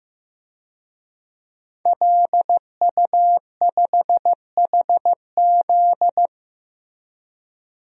Morse code, heard as a keyed tone, about 16 wpm, 700 Hz. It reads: LU5HZ
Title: DXCC-100 Call Signs - 5 Characters - Repeated 2x - 15wpm